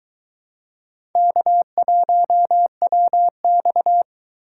Morse code, heard as a keyed tone, about 23 wpm, 700 Hz. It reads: K1WX